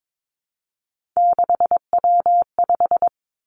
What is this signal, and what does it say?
Morse code, heard as a keyed tone, about 22 wpm, 700 Hz. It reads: 6W5
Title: Three Challenging Letter-Number Combinations 22wpm